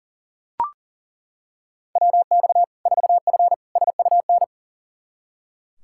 Morse code, heard as a keyed tone, about 40 wpm, 700 Hz. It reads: WX 4F SUN
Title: Practice QSO - Exchange 2 - Info - 40wpm